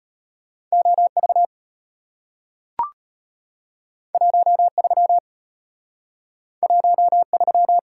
Morse code, heard as a keyed tone, about 34 wpm, 700 Hz. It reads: OV 13 13